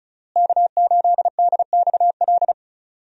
Morse code, heard as a keyed tone, about 34 wpm, 700 Hz. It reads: K8DXL
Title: USA Call Signs - 5 Characters - Repeated 3x - 35wpm